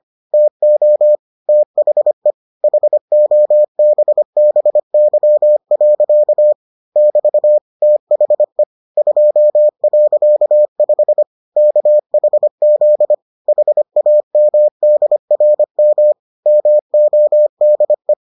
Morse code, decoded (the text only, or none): TO THE HOBBY. <BT> THE 2.5 KHZ HAMDRM MODE